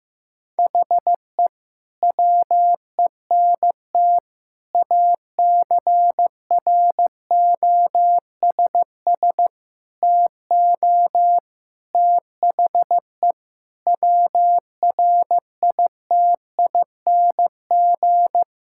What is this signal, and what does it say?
Morse code, heard as a keyed tone, about 15 wpm, 700 Hz. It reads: HE WENT ACROSS TO THE WRITING
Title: Sentences from Top 1000 Words - Review Part 5 - 15wpm